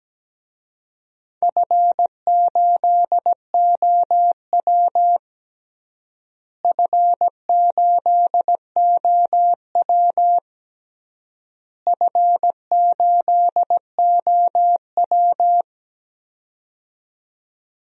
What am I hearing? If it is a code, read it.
Morse code, heard as a keyed tone, about 17 wpm, 700 Hz. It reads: F8OW F8OW F8OW
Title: DXCC-100 Call Signs - 4 Characters - Repeated 3x - 17wpm